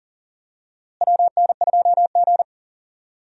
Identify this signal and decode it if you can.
Morse code, heard as a keyed tone, about 40 wpm, 700 Hz. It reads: WN2G
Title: USA Call Signs - 4 Characters - Repeated 2x - 40wpm